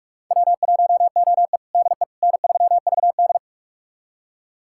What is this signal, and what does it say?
Morse code, heard as a keyed tone, about 46 wpm, 700 Hz. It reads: W1OE DE N3VD
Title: USA Callsign DE Callsign - 4 Characters - 45wpm